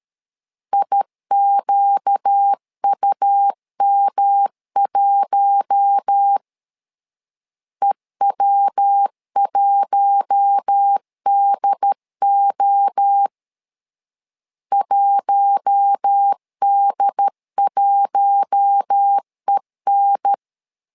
Morse code, heard as a keyed tone, about 13 wpm, 800 Hz. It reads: IQUM1 EW1DO 1D1EN
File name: LEZIONI / 4 Q1 12wpm 2028